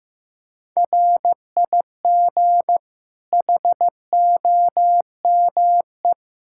Morse code, heard as a keyed tone, about 15 wpm, 700 Hz. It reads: RIG HOME
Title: Practice QSO - Exchange 2 - Info - 15wpm